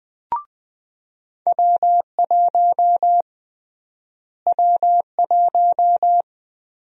Morse code, heard as a keyed tone, about 20 wpm, 700 Hz. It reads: W1 W1
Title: USA Call Sign Prefixes - Repeated 2x - 20wpm